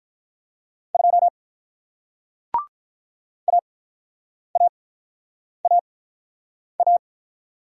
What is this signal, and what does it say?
Morse code, heard as a keyed tone, about 46 wpm, spaced as for 9 wpm, 700 Hz. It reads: 2 AAAA